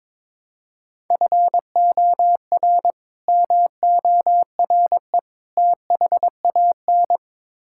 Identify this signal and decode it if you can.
Morse code, heard as a keyed tone, about 22 wpm, 700 Hz. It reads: FOR MORE THAN
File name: practice / 22wpm / Build A Phrase - 5-Word Phrases - 22wpm